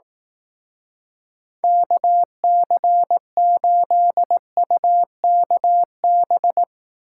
Morse code, heard as a keyed tone, about 19 wpm, 700 Hz. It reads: KC8UKB